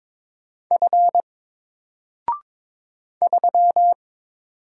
Morse code, heard as a keyed tone, about 23 wpm, 700 Hz. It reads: F 3
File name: Single-Letter-Number-22wpm